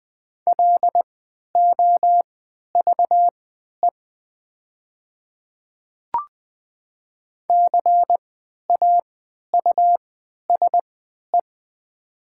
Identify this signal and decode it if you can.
Morse code, heard as a keyed tone, about 20 wpm, spaced as for 11 wpm, 700 Hz. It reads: LOVE CAUSE
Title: 3x Character Spacing - Top 500 Words - 20wpm